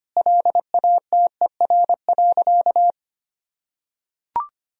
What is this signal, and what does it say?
Morse code, heard as a keyed tone, about 25 wpm, 700 Hz. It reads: LATER.
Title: Sentences from Top 1000 Words - Review Part 3 - 25wpm